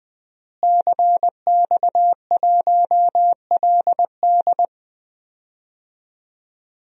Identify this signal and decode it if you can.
Morse code, heard as a keyed tone, about 20 wpm, 700 Hz. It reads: CX1LD
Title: DXCC-100 Call Signs - 5 Characters - Repeated 3x - 20wpm